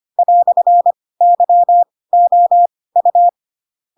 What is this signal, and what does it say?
Morse code, heard as a keyed tone, about 25 wpm, spaced as for 18 wpm, 700 Hz. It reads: "YOU
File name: Pinocchio / Pinocchio ch7 12 25 ik1hge